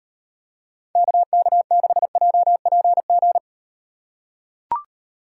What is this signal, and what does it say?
Morse code, heard as a keyed tone, about 38 wpm, 700 Hz. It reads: KK6JPG